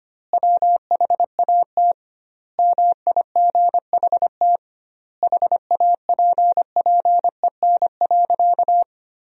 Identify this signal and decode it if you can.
Morse code, heard as a keyed tone, about 25 wpm, 700 Hz. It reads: WHAT MIGHT HAPPEN.
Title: 2x Word-Spacing - Sentences from Top 400 Words - 25wpm